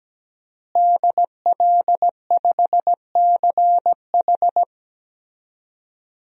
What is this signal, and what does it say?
Morse code, heard as a keyed tone, about 17 wpm, 700 Hz. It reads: DL5CH
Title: DXCC-100 Call Signs - 5 Characters - Repeated 3x - 17wpm